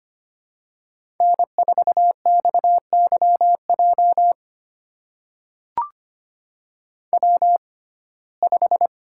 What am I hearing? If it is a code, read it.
Morse code, heard as a keyed tone, about 25 wpm, spaced as for 18 wpm, 700 Hz. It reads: N4XYJ W5
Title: USA Call Signs - 6x Character Spacing - 25wpm